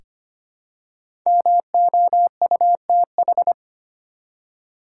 Morse code, heard as a keyed tone, about 25 wpm, 700 Hz. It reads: MOUTH
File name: 5 Letter Words - First 3 Letters Given - 25wpm